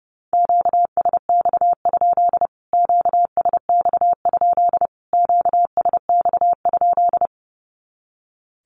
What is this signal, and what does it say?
Morse code, heard as a keyed tone, about 30 wpm, 700 Hz. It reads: QSX? QSX? QSX?
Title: Q-Codes Repeated 3x 30wpm